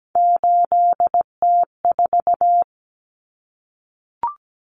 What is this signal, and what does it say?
Morse code, heard as a keyed tone, about 17 wpm, 700 Hz. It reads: 8T4
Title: Common Cut Numbers - 3-Digit 17wpm